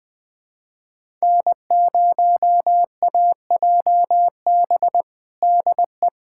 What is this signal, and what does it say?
Morse code, heard as a keyed tone, about 21 wpm, 700 Hz. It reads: N0AJB DE
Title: USA Callsign DE Callsign - 5 Characters - 20wpm